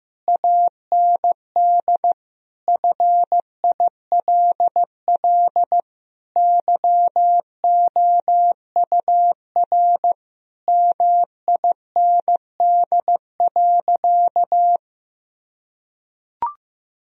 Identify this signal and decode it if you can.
Morse code, heard as a keyed tone, about 15 wpm, 700 Hz. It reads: AND FILL YOUR MIND.
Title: Sentences from Top 1000 Words - Review Part 3 - 15wpm